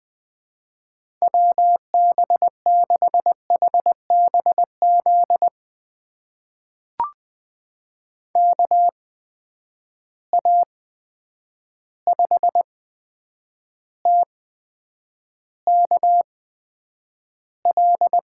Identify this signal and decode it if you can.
Morse code, heard as a keyed tone, about 20 wpm, spaced as for 9 wpm, 700 Hz. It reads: WB6HBZ KA5TKL
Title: USA Call Signs - 6 Characters - 8x Character Spacing - 20wpm